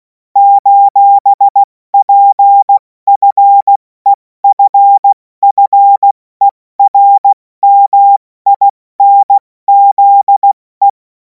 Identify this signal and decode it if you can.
Morse code, heard as a keyed tone, about 16 wpm, 800 Hz. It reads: :PFEFFERMINZE